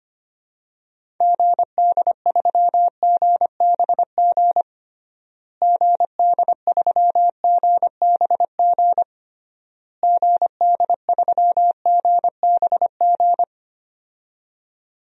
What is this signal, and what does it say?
Morse code, heard as a keyed tone, about 25 wpm, 700 Hz. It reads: GD3GBG GD3GBG GD3GBG
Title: DXCC-100 Call Signs - 6 Characters - Repeated 3x - 25wpm